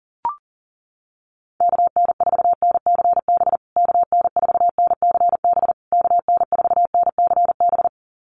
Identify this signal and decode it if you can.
Morse code, heard as a keyed tone, about 40 wpm, 700 Hz. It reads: KN4NCB KN4NCB KN4NCB
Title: USA Call Signs Repeated 3x 40wpm